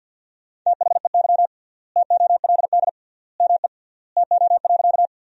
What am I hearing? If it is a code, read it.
Morse code, heard as a keyed tone, about 48 wpm, 700 Hz. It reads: THEY TOLD ME TO.
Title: 3x Word-Spacing - Sentences from Top 400 Words - 50wpm